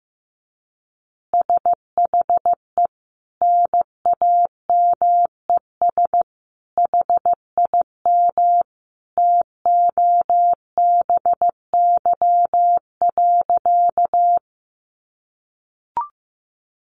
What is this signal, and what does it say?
Morse code, heard as a keyed tone, about 15 wpm, 700 Hz. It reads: SHE NAMES HIM TOBY.
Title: Easy Phrases 15wpm